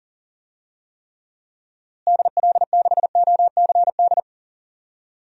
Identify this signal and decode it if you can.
Morse code, heard as a keyed tone, about 40 wpm, 700 Hz. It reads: DL6OCD
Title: DXCC-100 Call Signs - 6 Characters - Repeated 3x - 40wpm